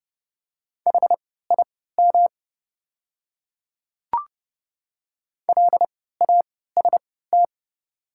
Morse code, heard as a keyed tone, about 30 wpm, spaced as for 17 wpm, 700 Hz. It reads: HIM LAST
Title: 3x Character Spacing - Top 500 Words - 30wpm